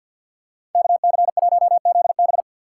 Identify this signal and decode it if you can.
Morse code, heard as a keyed tone, about 47 wpm, 700 Hz. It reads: KC1ZB
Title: USA Call Signs - 4x Character Spacing - 50wpm